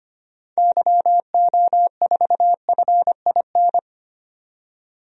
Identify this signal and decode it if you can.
Morse code, heard as a keyed tone, about 25 wpm, 700 Hz. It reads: YO4FIN
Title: DXCC-100 Call Signs - 6 Characters - 25wpm